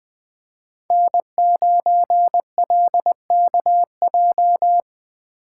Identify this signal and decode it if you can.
Morse code, heard as a keyed tone, about 23 wpm, 700 Hz. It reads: N9LKJ
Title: Speed Racing - USA 1x3 Call Signs 20wpm